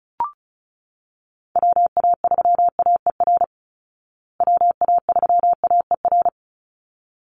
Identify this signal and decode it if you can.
Morse code, heard as a keyed tone, about 34 wpm, 700 Hz. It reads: WA3AER WA3AER
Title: Call Signs Repeated 3x - 35wpm